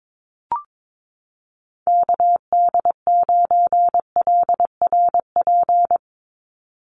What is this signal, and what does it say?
Morse code, heard as a keyed tone, about 22 wpm, 700 Hz. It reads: KD9LRP
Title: USA Call Signs 22wpm